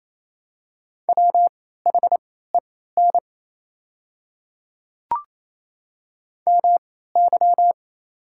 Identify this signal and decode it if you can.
Morse code, heard as a keyed tone, about 28 wpm, spaced as for 16 wpm, 690 Hz. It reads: WHEN MY
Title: 3x Character Spacing - Top 100 Words - 28wpm